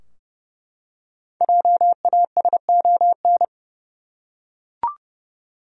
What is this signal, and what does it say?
Morse code, heard as a keyed tone, about 30 wpm, 700 Hz. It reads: JASON
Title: USA Common Names - 30wpm